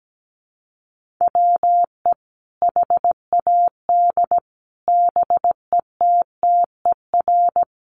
Morse code, heard as a keyed tone, about 17 wpm, 700 Hz. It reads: WE HAD BETTER
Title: Sentences from Top 1400 Words 17wpm